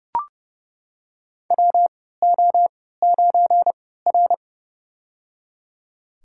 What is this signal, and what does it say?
Morse code, heard as a keyed tone, about 30 wpm, spaced as for 17 wpm, 700 Hz. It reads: WO9R
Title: USA Call Signs - 3x Character Spacing - 30wpm